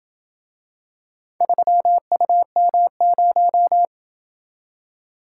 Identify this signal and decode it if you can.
Morse code, heard as a keyed tone, about 27 wpm, 700 Hz. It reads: 3UM0